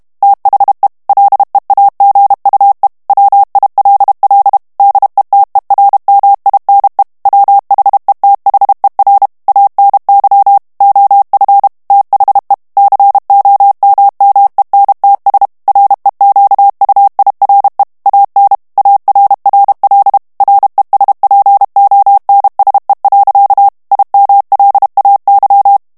Morse code, heard as a keyed tone, about 32 wpm, 800 Hz. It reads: THE LEAGUE WILL DETERMINE WHETHER ANY OF THE COMMENTS REQUIRE AN ARRL RESPONSE. IMLAY
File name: ANGLAIS texte clair 3 vitesse 30 mots minute